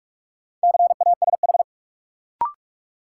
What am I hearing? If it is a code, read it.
Morse code, heard as a keyed tone, about 46 wpm, 700 Hz. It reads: CASH
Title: Sentences from Top 700 Words - New Word Drill - 45wpm